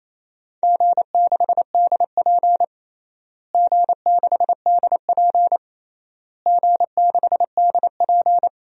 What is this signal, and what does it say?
Morse code, heard as a keyed tone, about 28 wpm, 700 Hz. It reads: G6DP G6DP G6DP
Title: DXCC-100 Call Signs - 4 Characters - Repeated 3x - 28wpm